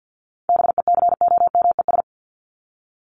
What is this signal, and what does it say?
Morse code, heard as a keyed tone, about 50 wpm, 700 Hz. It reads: BECOMES
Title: ES:S Suffix - Top 100 Words with Base Words 50wpm